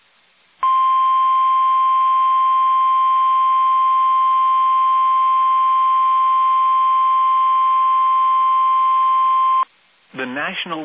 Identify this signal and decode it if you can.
Emergency Alert System (SAME) tones (the 1050 Hz attention tone), with speech. It is heard from 0.6 to 9.7 s.